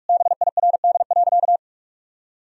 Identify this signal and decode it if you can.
Morse code, heard as a keyed tone, about 46 wpm, 700 Hz. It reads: BIRD.
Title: Sentences from Top 1000 Words - Review Part 2 - 45wpm